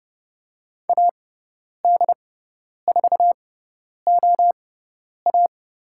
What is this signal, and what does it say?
Morse code, heard as a keyed tone, about 30 wpm, spaced as for 10 wpm, 700 Hz. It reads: AD4OA
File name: Lesson 16g - Letter c - Call Signs 30wpm